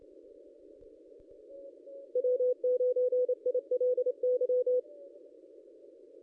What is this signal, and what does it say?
Morse code, heard as a keyed tone, about 28 wpm, 500 Hz. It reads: W9ILY
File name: contests / ss-cw-k5tr-op-n5rz -2200